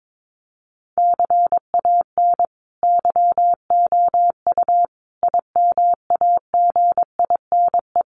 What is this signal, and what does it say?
Morse code, heard as a keyed tone, about 22 wpm, 700 Hz. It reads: CAN YOU IMAGINE
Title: Sentences from Top 1400 Words 22wpm